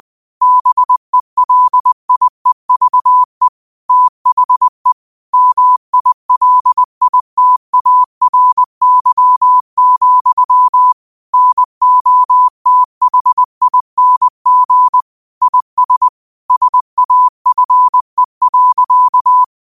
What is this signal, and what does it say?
Morse code, heard as a keyed tone, about 20 wpm, 1000 Hz. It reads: BELIEVE THE MILITARY, NOTHING IS SAFE.
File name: qotdfiles / 20wpm